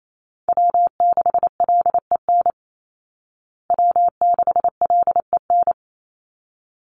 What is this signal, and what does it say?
Morse code, heard as a keyed tone, about 28 wpm, 700 Hz. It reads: W6LEN W6LEN
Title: Call Signs Repeated 3x - 28wpm